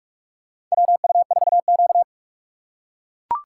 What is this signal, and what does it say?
Morse code, heard as a keyed tone, about 42 wpm, 700 Hz. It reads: WU4Q
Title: USA Call Signs - 4 Characters - 3x Character Spacing - 45wpm